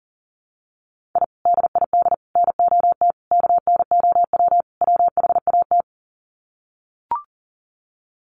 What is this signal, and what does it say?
Morse code, heard as a keyed tone, about 40 wpm, 700 Hz. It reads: I DID NOT KNOW WHAT
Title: Sets of 5 Words 40wpm